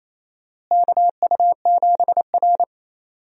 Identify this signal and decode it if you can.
Morse code, heard as a keyed tone, about 28 wpm, 700 Hz. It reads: KU7R
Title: USA Call Signs - 4 Characters - Repeated 2x - 28wpm